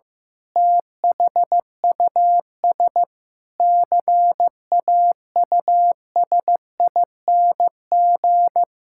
Morse code, heard as a keyed tone, about 15 wpm, 700 Hz. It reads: THUS CAUSING
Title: Pangram Challenge v2 - 15wpm